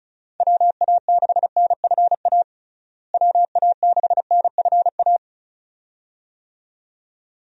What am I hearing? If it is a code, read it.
Morse code, heard as a keyed tone, about 36 wpm, 700 Hz. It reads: WA6NFA WA6NFA
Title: USA Call Signs - 6 Characters - Repeated 2x - 35wpm